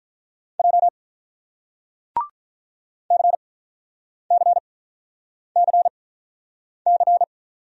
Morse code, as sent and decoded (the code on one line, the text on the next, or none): .--- / -.-. -.-. -.-. -.-.
J CCCC